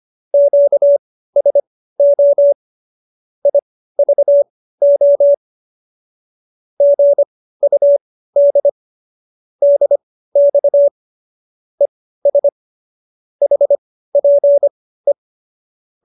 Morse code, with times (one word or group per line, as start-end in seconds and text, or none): 0.3-2.5 QSO
3.5-5.3 IVO
6.8-8.7 GUD
9.6-10.9 DX
11.8-12.5 ES
13.4-15.1 HPE